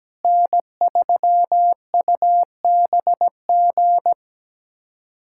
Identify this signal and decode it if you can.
Morse code, heard as a keyed tone, about 17 wpm, 700 Hz. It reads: N3UBG
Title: USA Callsign DE Callsign - 5 Characters - 17wpm